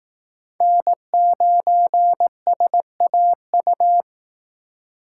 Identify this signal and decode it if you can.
Morse code, heard as a keyed tone, about 18 wpm, 700 Hz. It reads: N9SAU
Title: Speed Racing - USA Call Signs 15wpm